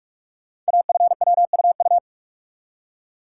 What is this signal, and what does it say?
Morse code, heard as a keyed tone, about 44 wpm, 700 Hz. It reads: AFWUU
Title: Code Groups - 5 Letters - 45wpm